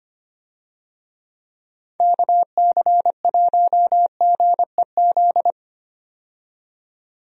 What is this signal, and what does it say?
Morse code, heard as a keyed tone, about 25 wpm, 700 Hz. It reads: KC1GEZ
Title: USA Call Signs - 6 Characters - 25wpm